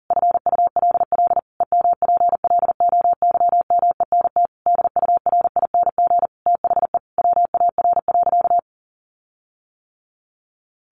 Morse code, heard as a keyed tone, about 40 wpm, 700 Hz. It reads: FULL EMPLOYMENT DURING THE WAR.
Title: Sentences from Top 1500 Words 40wpm